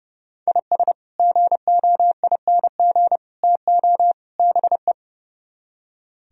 Morse code, heard as a keyed tone, about 37 wpm, 700 Hz. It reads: IS GOING TO BE
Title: Speed-Racing - Sets of 4 Words 30wpm